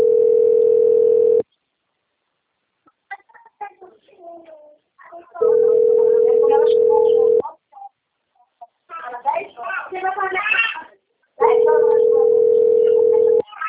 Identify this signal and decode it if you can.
Ringback tone (North America), 440 + 480 Hz, with speech.
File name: rg-1600-unknown- - - .41006